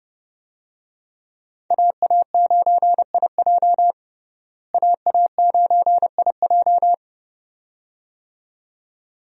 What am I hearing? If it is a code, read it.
Morse code, heard as a keyed tone, about 30 wpm, 700 Hz. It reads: AA9IJ AA9IJ
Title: USA Call Signs - 5 Characters - Repeated 2x - 30wpm